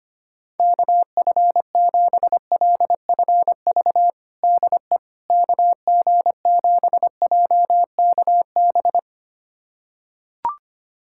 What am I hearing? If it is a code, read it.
Morse code, heard as a keyed tone, about 25 wpm, 700 Hz. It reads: KF7LFV DE KG7JKB
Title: USA Callsign DE Callsign 25wpm